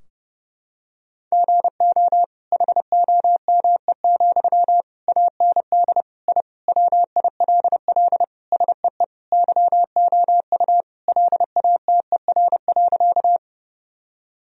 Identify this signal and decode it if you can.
Morse code, heard as a keyed tone, about 31 wpm, 700 Hz. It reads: GO HOME, AND I WILL SEE YOU LATER.
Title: Sentences from Top 1000 Words - Review Part 4 - 30wpm